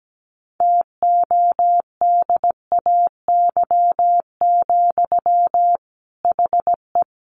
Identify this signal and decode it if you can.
Morse code, heard as a keyed tone, about 17 wpm, 700 Hz. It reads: TODAY, HE
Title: Sentences from Top 600 Words 17wpm